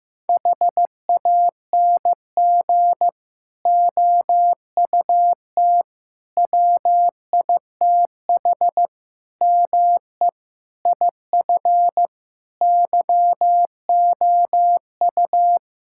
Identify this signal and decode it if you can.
Morse code, heard as a keyed tone, about 15 wpm, 700 Hz. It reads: HANG OUT WITH ME IF YOU
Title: Sentences from Top 1600 Words 15wpm